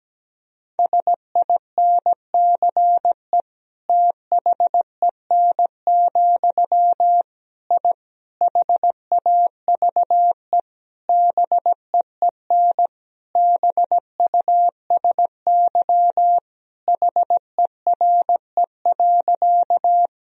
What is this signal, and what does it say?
Morse code, heard as a keyed tone, about 17 wpm, 700 Hz. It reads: SINCE THEN, I HAVE BEEN BUSY HERE.
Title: Sentences from Top 1000 Words - Review Part 3 - 17wpm